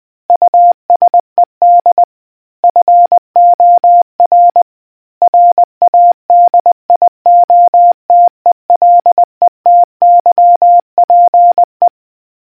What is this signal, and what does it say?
Morse code, heard as a keyed tone, about 20 wpm, 700 Hz. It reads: USED FOR RADIOTELETYPE